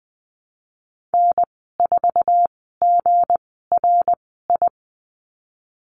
Morse code, heard as a keyed tone, about 20 wpm, spaced as for 15 wpm, 700 Hz. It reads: N4GRI
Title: 179 Lesson 29g - Slash - Call Signs 20wpm